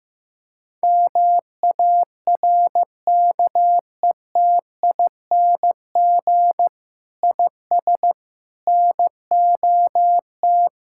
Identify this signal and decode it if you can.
Morse code, heard as a keyed tone, about 15 wpm, 700 Hz. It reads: MARKETING IS NOT
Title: Sentences from Top 1000 Words - Review Part 5 - 15wpm